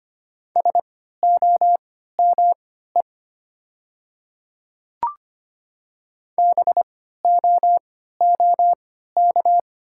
Morse code, heard as a keyed tone, about 25 wpm, spaced as for 14 wpm, 700 Hz. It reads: SOME BOOK
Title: 3x Character Spacing - Top 300 Words - 25wpm